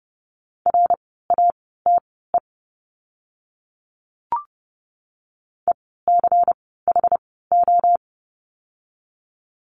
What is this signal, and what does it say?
Morse code, heard as a keyed tone, about 30 wpm, spaced as for 17 wpm, 700 Hz. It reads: RATE ECHO
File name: Lesson 16f - Letter c - Words 30wpm